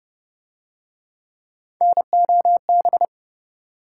Morse code, heard as a keyed tone, about 26 wpm, 700 Hz. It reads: NOB